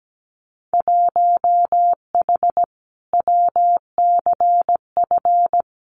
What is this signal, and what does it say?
Morse code, heard as a keyed tone, about 17 wpm, 700 Hz. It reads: 1H WCF
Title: Winter Field Day Exchanges 17wpm